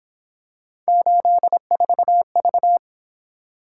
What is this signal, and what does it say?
Morse code, heard as a keyed tone, about 26 wpm, 700 Hz. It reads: 84V
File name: Speed-Racing - Three Letter-Number Combinations - 22wpm